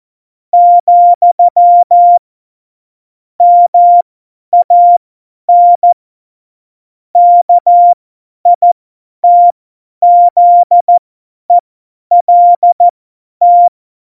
Morse code, heard as a keyed tone, about 14 wpm, spaced as for 10 wpm, 700 Hz. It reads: , MAN KITZELT
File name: Morsekurs / Witze / Witz 02. Oktober 2018, 50 F70